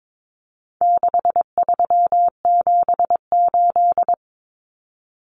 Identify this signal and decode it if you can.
Morse code, heard as a keyed tone, about 22 wpm, 700 Hz. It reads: 6378
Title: Numbers - 4-Digit 22wpm